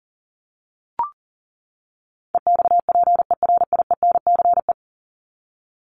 Morse code, heard as a keyed tone, about 40 wpm, 700 Hz. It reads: EXPERIENCE
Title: Top 501-750 Words 40wpm